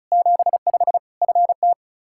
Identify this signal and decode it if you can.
Morse code, heard as a keyed tone, about 36 wpm, 700 Hz. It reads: 75 FT
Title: Practice QSO - Exchange 2 - Info - 35wpm